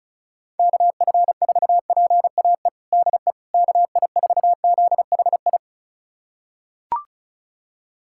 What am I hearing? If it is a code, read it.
Morse code, heard as a keyed tone, about 36 wpm, 700 Hz. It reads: KF4PAE DE KI4ZHI
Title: USA Callsign DE Callsign - 6 Characters - 35wpm